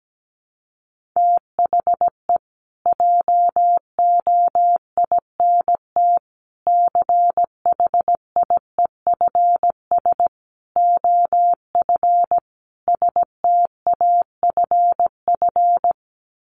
Morse code, heard as a keyed tone, about 17 wpm, 700 Hz. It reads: THE JOINT CHIEFS OF STAFF
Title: Sets of 5 Words Encore2 17wpm